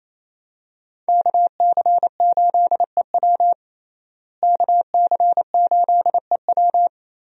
Morse code, heard as a keyed tone, about 28 wpm, 700 Hz. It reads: KC8EW KC8EW
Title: USA Call Signs - 5 Characters - Repeated 3x - 28wpm